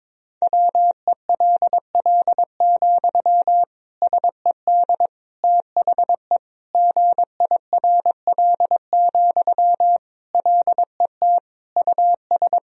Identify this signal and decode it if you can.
Morse code, heard as a keyed tone, about 22 wpm, 700 Hz. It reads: WELL, SED THE GIRL, LET US
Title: Sentences from Top 400 Words and Abbreviations 22wpm